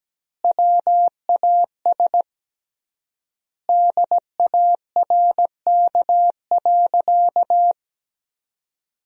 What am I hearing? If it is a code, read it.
Morse code, heard as a keyed tone, about 17 wpm, 700 Hz. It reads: WAS DARK.
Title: 3x Word-Spacing - Sentences from Top 500 Words - 17wpm